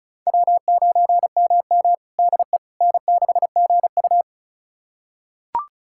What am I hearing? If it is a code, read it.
Morse code, heard as a keyed tone, about 36 wpm, 700 Hz. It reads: W9MM DE N6GU
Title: USA Callsign DE Callsign - 4 Characters - 35wpm